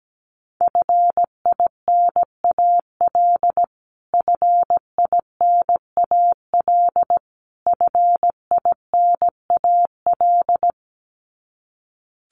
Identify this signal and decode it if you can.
Morse code, heard as a keyed tone, about 17 wpm, 700 Hz. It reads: FINAL FINAL FINAL
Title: Top 500 Words Repeated 17wpm